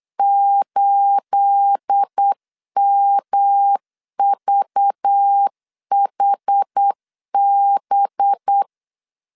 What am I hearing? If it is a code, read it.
Morse code, heard as a keyed tone, about 8 wpm, 800 Hz. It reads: 8MVHB
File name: CW / DET / 1 9 1500C S B 8wpm 5624